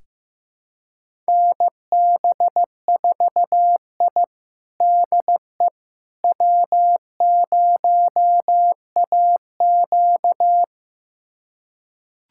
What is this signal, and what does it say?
Morse code, heard as a keyed tone, about 16 wpm, 700 Hz. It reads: NB4I DE W0AQ
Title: USA Callsign DE Callsign - 4 Characters - 15wpm